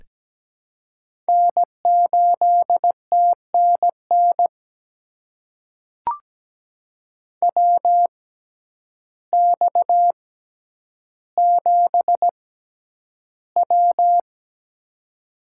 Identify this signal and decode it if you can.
Morse code, heard as a keyed tone, about 17 wpm, spaced as for 9 wpm, 700 Hz. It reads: N8TNN WX7W
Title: USA Call Signs - 6x Character Spacing - 17wpm